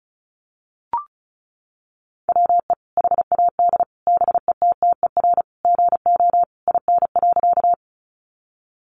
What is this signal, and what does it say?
Morse code, heard as a keyed tone, about 34 wpm, 700 Hz. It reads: WE HAD BETTER GO IN.
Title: Sentences from Top 400 Words 35wpm